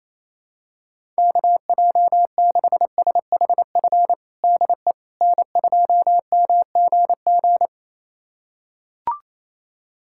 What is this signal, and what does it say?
Morse code, heard as a keyed tone, about 28 wpm, 700 Hz. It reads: KJ6SHF DE N2MGG
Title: USA Callsign DE Callsign 28wpm